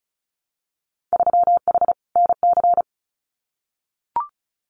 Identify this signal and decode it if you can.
Morse code, heard as a keyed tone, about 34 wpm, 700 Hz. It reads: 3H NC
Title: Winter Field Day Exchanges 35wpm